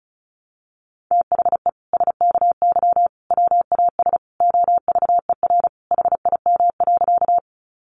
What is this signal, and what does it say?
Morse code, heard as a keyed tone, about 34 wpm, 700 Hz. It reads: THE SKY WAS OVER HIM.
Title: Sentences from Top 1200 Words 35wpm